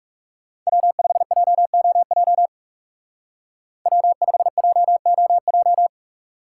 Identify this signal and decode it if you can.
Morse code, heard as a keyed tone, about 37 wpm, 700 Hz. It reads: W5JOJ W5JOJ
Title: Speed Racing - USA 1x3 Call Signs 30wpm